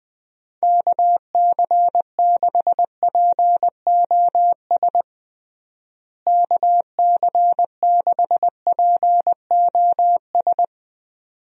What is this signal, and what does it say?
Morse code, heard as a keyed tone, about 20 wpm, 700 Hz. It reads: KC6POS KC6POS
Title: USA Call Signs - 6 Characters - Repeated 2x - 20wpm